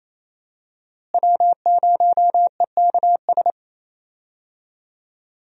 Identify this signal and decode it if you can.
Morse code, heard as a keyed tone, about 28 wpm, 700 Hz. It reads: W0EKS
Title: USA Call Signs - 5 Characters - Repeated 3x - 28wpm